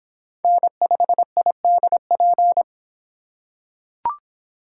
Morse code, heard as a keyed tone, about 25 wpm, 700 Hz. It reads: N5IDP